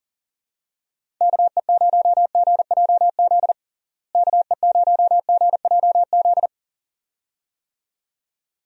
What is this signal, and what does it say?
Morse code, heard as a keyed tone, about 40 wpm, 700 Hz. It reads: KE0GJZ KE0GJZ
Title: USA Call Signs - 6 Characters - Repeated 2x - 40wpm